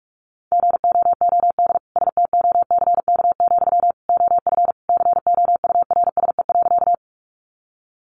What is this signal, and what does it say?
Morse code, heard as a keyed tone, about 44 wpm, 700 Hz. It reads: GOOD STOCK, OF COURSE.
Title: Sentences from Top 800 Words 45wpm